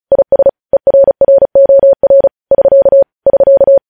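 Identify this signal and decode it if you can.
Morse code, heard as a keyed tone, about 36 wpm, 550 Hz. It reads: IS ERROR <SK> <SK>
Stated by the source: Morse code news headlines sent at 35, beginning with vvv.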